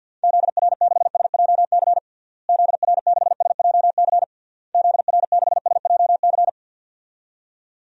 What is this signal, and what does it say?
Morse code, heard as a keyed tone, about 46 wpm, 700 Hz. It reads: ZR6SJC ZR6SJC ZR6SJC
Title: DXCC-100 Call Signs - 6 Characters - Repeated 3x - 50wpm